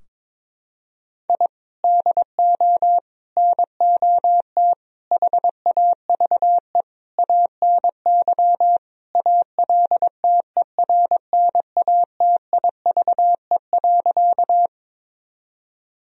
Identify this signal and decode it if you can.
Morse code, heard as a keyed tone, about 22 wpm, 700 Hz. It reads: I DO NOT HAVE ANY ALTERNATIVE.
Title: Sentences from Top 1000 Words - Review Part 4 - 22wpm